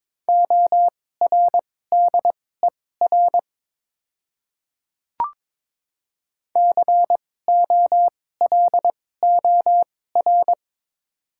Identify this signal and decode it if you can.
Morse code, heard as a keyed tone, about 22 wpm, spaced as for 16 wpm, 700 Hz. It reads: ORDER COLOR
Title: 2x Character Spacing - Top 500 Words - 22wpm